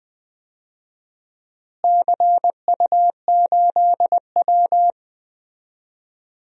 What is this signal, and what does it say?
Morse code, heard as a keyed tone, about 21 wpm, 700 Hz. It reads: CU8W
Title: DXCC-100 Call Signs - 4 Characters - Repeated 3x - 20wpm